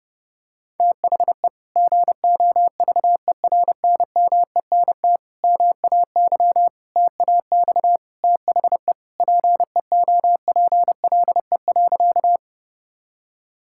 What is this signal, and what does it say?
Morse code, heard as a keyed tone, about 30 wpm, 700 Hz. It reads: THE GOVERNMENT MAY TAX THE PEOPLE.
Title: Sentences from Top 1000 Words - Review Part 5 - 30wpm